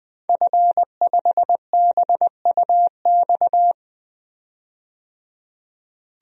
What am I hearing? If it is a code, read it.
Morse code, heard as a keyed tone, about 20 wpm, 700 Hz. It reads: F5BUX
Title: DXCC-100 Call Signs - 5 Characters - 20wpm